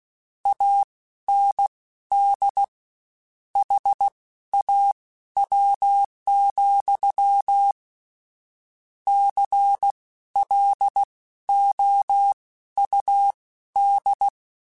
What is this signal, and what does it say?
Morse code, heard as a keyed tone, about 16 wpm, spaced as for 12 wpm, 800 Hz. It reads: AND HAW, CLOUD
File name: Misc / Misc-mp3s / WordPairs sp @15